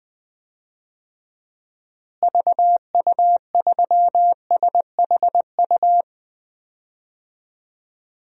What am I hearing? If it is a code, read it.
Morse code, heard as a keyed tone, about 20 wpm, 700 Hz. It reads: VU3SHU
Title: DXCC-100 Call Signs - 6 Characters - Repeated 2x - 20wpm